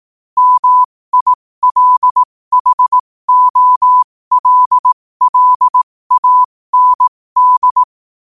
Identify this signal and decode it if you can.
Morse code, heard as a keyed tone, about 18 wpm, spaced as for 15 wpm, 1000 Hz. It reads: MILHOLLAND